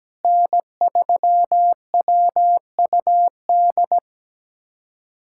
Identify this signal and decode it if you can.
Morse code, heard as a keyed tone, about 17 wpm, 700 Hz. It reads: N3WUD
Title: USA Callsign DE Callsign - 5 Characters - 17wpm